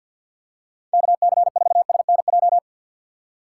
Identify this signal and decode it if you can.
Morse code, heard as a keyed tone, about 51 wpm, 700 Hz. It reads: KC4SNJ
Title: USA Call Signs - 6 Characters - 4x Character Spacing - 50wpm